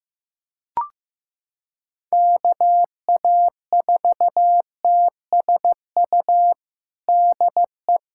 Morse code, heard as a keyed tone, about 15 wpm, 700 Hz. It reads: KA4TSU DE
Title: USA Callsign DE Callsign - 6 Characters - 15wpm